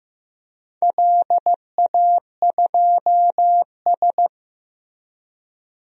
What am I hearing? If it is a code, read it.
Morse code, heard as a keyed tone, about 15 wpm, 700 Hz. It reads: LA2S
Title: DXCC-100 Call Signs - 4 Characters - Repeated 2x - 15wpm